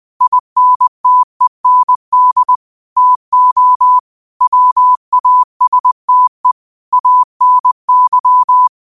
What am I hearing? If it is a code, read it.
Morse code, heard as a keyed tone, about 20 wpm, 1000 Hz. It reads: INTEND TO WASTE ANY
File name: qotdfiles / 20wpm